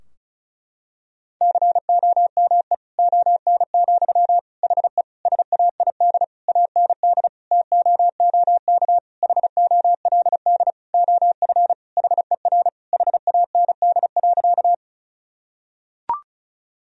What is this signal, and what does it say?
Morse code, heard as a keyed tone, about 34 wpm, 700 Hz. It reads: COME ON, HE SAID AND TOOK HOLD OF HER HAND.
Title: Sentences from Top 1000 Words - Review Part 2 - 35wpm